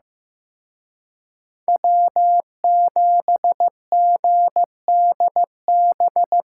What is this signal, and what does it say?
Morse code, heard as a keyed tone, about 18 wpm, 700 Hz. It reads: W7GDB
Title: Speed Racing - USA Call Signs 15wpm